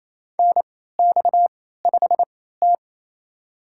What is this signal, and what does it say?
Morse code, heard as a keyed tone, about 28 wpm, spaced as for 16 wpm, 700 Hz. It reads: NX5T
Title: USA Call Signs - 4 Characters - 3x Character Spacing - 28wpm